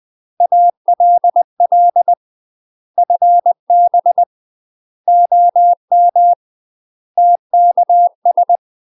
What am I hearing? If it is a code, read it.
Morse code, heard as a keyed tone, about 20 wpm, 700 Hz. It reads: ALL FB OM TKS
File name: UK MASTER QSOs TRAINING/20WPM